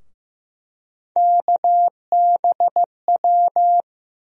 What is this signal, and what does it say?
Morse code, heard as a keyed tone, about 15 wpm, 700 Hz. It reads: KBW
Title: USA Call Sign Postfixes - 3 Letters 15wpm